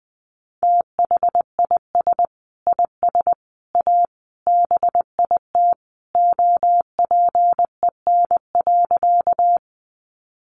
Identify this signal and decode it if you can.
Morse code, heard as a keyed tone, about 20 wpm, 700 Hz. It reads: THIS IS A BIT OPEN.
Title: Sentences from Top 900 Words 20wpm